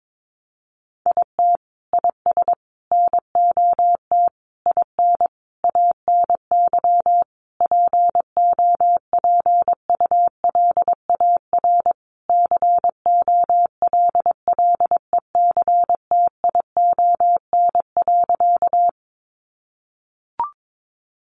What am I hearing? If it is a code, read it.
Morse code, heard as a keyed tone, about 22 wpm, 700 Hz. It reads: IT IS NOT IN ANY POPULAR COLLECTION.
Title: Sentences from Top 1200 Words 22wpm